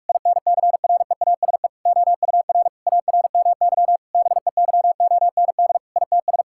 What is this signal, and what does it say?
Morse code, heard as a keyed tone, about 46 wpm, 700 Hz. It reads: INCREASE OUR ARMY BEYOND ITS
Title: Sentences from Top 1000 Words - Review Part 6 - 45wpm